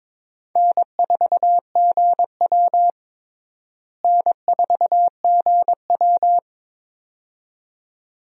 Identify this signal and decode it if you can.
Morse code, heard as a keyed tone, about 22 wpm, 700 Hz. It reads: N4GW N4GW
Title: USA Call Signs - 4 Characters - Repeated 2x - 22wpm